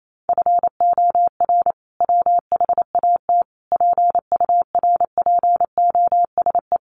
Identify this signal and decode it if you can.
Morse code, heard as a keyed tone, about 28 wpm, 700 Hz. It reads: FOR WHAT PURPOSE
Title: Sentences from Top 800 Words 28wpm